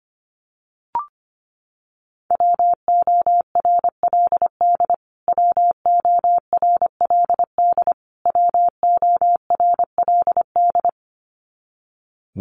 Morse code, heard as a keyed tone, about 25 wpm, 700 Hz. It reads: WORLD WORLD WORLD
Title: Top 200 Words Repeated 25wpm